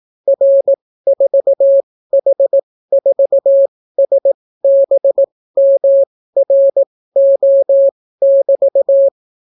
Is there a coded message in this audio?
Yes: Morse code from 0.3 to 9.1 s, heard as a keyed tone, about 18 wpm, spaced as for 14 wpm, 550 Hz. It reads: R4H4SBMRO<BT>